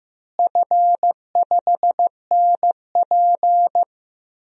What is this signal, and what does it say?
Morse code, heard as a keyed tone, about 15 wpm, 700 Hz. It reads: F5NP
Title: DXCC-100 Call Signs - 4 Characters - Repeated 3x - 15wpm